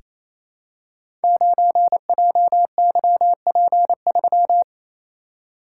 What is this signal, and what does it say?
Morse code, heard as a keyed tone, about 28 wpm, 700 Hz. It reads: 9JYP3
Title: Code Groups - 5 Letter:Numbers - 28wpm